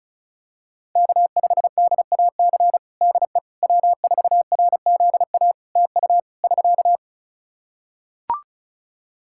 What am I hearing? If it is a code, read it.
Morse code, heard as a keyed tone, about 34 wpm, 700 Hz. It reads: K5DAC DE W4RZA TU <SK>
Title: Practice QSO - Exchange 4 - Ending - 35wpm